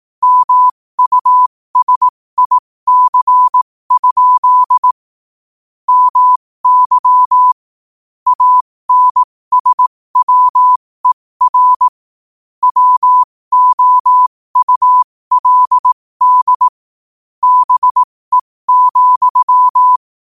Morse code, heard as a keyed tone, about 18 wpm, spaced as for 15 wpm, 1000 Hz. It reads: MUSIC? MY ANSWER WOULD BE,
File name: qotdfiles / 15wpm